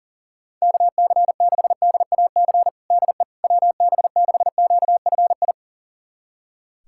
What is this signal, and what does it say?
Morse code, heard as a keyed tone, about 40 wpm, 700 Hz. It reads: KC6DAC DE WB6QFI
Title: USA Callsign DE Callsign - 6 Characters - 40wpm